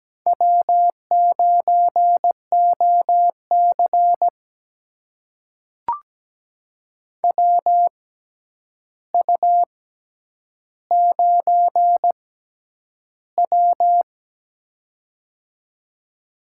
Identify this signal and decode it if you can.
Morse code, heard as a keyed tone, about 17 wpm, spaced as for 9 wpm, 700 Hz. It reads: W9OC WU9W
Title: USA Call Signs - 4 Characters - 6x Character Spacing - 17wpm